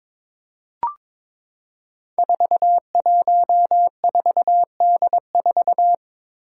Morse code, heard as a keyed tone, about 22 wpm, 700 Hz. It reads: 414D4
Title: Five Challenging Letter-Number Combinations 22wpm